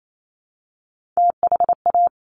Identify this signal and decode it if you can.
Morse code, heard as a keyed tone, about 28 wpm, 700 Hz. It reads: THA
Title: Top Three Letter Combinations in Words 28wpm